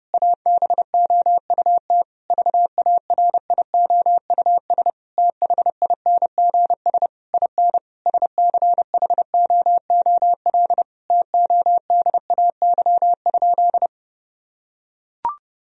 Morse code, heard as a keyed tone, about 30 wpm, 700 Hz. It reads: ABOUT VARIOUS THINGS IN SCHOOL TODAY?
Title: Sentences from Top 1000 Words - Review Part 4 - 30wpm